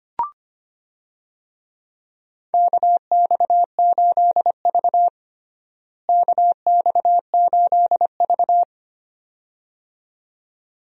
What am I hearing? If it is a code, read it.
Morse code, heard as a keyed tone, about 25 wpm, 700 Hz. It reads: KX8V KX8V
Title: USA Call Signs - 4 Characters - Repeated 2x - 25wpm